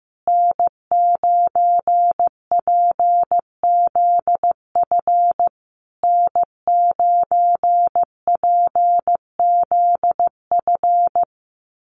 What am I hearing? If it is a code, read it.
Morse code, heard as a keyed tone, about 15 wpm, 700 Hz. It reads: N9PZF N9PZF
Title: USA Call Signs Repeated 3x 15wpm